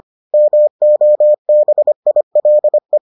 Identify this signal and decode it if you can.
Morse code, heard as a keyed tone, about 25 wpm, 600 Hz. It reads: MOBILE